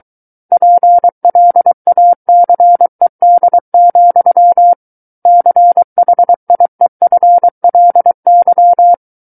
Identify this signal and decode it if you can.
Morse code, heard as a keyed tone, about 23 wpm, 700 Hz. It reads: PLACED, CHIEFLY
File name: Book One / 14-Chapter 14 - 23 WPM